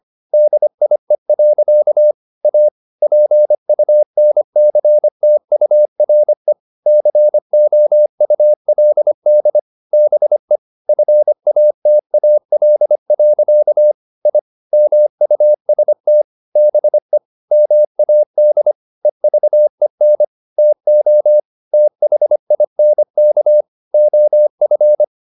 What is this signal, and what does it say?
Morse code, heard as a keyed tone, about 25 wpm, 600 Hz. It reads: DIE. A PUNCTURE COULD BE FATAL. I MUST BE MAD EVEN TO THINK OF